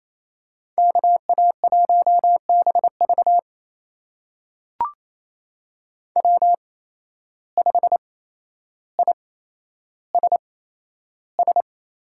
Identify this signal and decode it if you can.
Morse code, heard as a keyed tone, about 28 wpm, spaced as for 12 wpm, 700 Hz. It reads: KA1BV W5ISS
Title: USA Call Signs - 8x Character Spacing - 28wpm